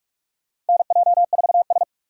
Morse code, heard as a keyed tone, about 46 wpm, 700 Hz. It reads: NJ4S
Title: USA Call Signs - 2x Character Spacing - 45wpm